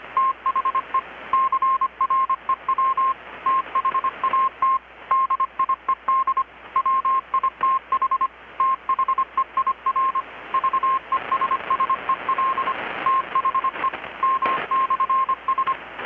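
Morse code, heard as a keyed tone, about 25 wpm, 1060 Hz, with radio noise. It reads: THE CREW THAT DIED WITH THEIR VESSEL THE M/S
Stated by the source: Morse from KPH's Night of Nights XX maritime radiotelegraphy event, received in Chiapas, Mexico about 3,800 km away on a Kenwood R-600 with a grounded 10 m long-wire antenna.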